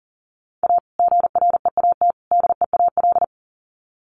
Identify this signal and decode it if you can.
Morse code, heard as a keyed tone, about 40 wpm, 700 Hz. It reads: A GREAT DEAL
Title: Sets of 3 Words 40wpm